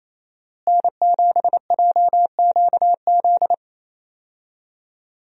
Morse code, heard as a keyed tone, about 28 wpm, 700 Hz. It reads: N7JQZ
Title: USA Call Signs - 5 Characters - Repeated 3x - 28wpm